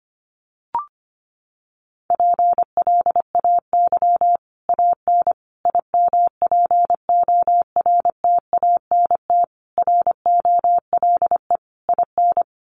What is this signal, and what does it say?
Morse code, heard as a keyed tone, about 25 wpm, 700 Hz. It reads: PLAY AN IMPORTANT ROLE IN
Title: Sets of 5 Words Encore 25wpm